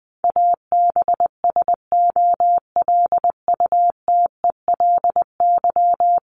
Morse code, heard as a keyed tone, about 20 wpm, 700 Hz. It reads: ABSOLUTELY
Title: LY Suffix - Top 100 Words with Base Words 20wpm